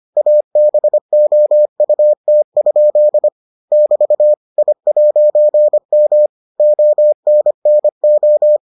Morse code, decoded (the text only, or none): ABOUT? <BT> I'M ONNO